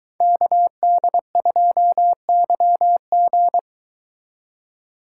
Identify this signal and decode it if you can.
Morse code, heard as a keyed tone, about 23 wpm, 700 Hz. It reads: KD2YG